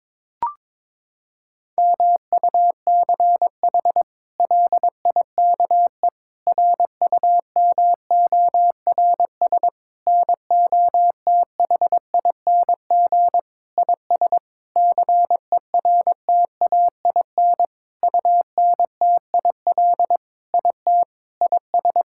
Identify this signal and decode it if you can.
Morse code, heard as a keyed tone, about 22 wpm, 700 Hz. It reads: MUCH LIKE RUMORS NOTHING IS CERTAIN UNTIL IT IS
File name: practice / 22wpm / Sentences with 8-Word Phrases - 22wpm